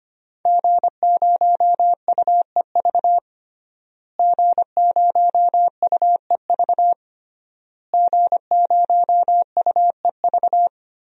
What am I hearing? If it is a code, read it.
Morse code, heard as a keyed tone, about 25 wpm, 700 Hz. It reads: G0UEV G0UEV G0UEV
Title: DXCC-100 Call Signs - 5 Characters - Repeated 3x - 25wpm